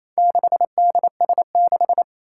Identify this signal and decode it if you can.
Morse code, heard as a keyed tone, about 27 wpm, 700 Hz. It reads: 6DS6
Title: Four Challenging Letter-Number Combinations 28wpm